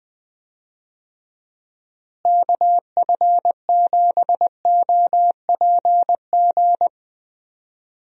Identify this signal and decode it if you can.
Morse code, heard as a keyed tone, about 20 wpm, 700 Hz. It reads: KF7OPG
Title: USA Call Signs - 6 Characters - Repeated 3x - 20wpm